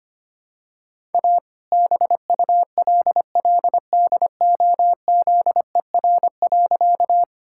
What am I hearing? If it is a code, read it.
Morse code, heard as a keyed tone, about 25 wpm, 700 Hz. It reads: A BULLDOZER.
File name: practice / 25wpm / All-Age Appropriate Jokes - 25wpm